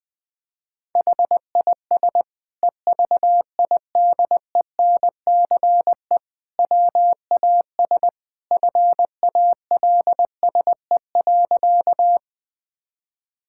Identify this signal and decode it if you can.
Morse code, heard as a keyed tone, about 20 wpm, 700 Hz. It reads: HIS EVIDENCE WAS FALSE.
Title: Sentences from Top 1600 Words 20wpm